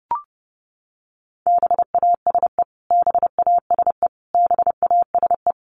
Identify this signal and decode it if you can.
Morse code, heard as a keyed tone, about 30 wpm, 700 Hz. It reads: BASE BASE BASE
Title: Top 300 Words Repeated 30wpm